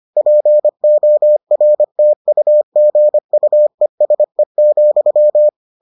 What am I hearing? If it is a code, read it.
Morse code, heard as a keyed tone, about 25 wpm, 600 Hz. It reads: PORTUGUESE,